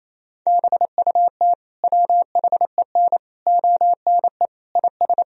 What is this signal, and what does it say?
Morse code, heard as a keyed tone, about 28 wpm, 700 Hz. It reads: BUT WHEN ONE IS
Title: Sentences from Top 1000 Words - Review Part 1 - 28wpm